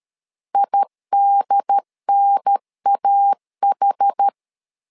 Morse code, heard as a keyed tone, about 13 wpm, 800 Hz. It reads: IDNAH
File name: LEZIONI / 4 Q1 12wpm 2028